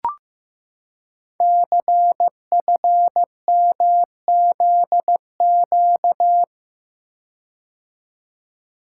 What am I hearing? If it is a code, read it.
Morse code, heard as a keyed tone, about 15 wpm, 700 Hz. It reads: CFMZQ
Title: Code Groups - 5 Letters - 15wpm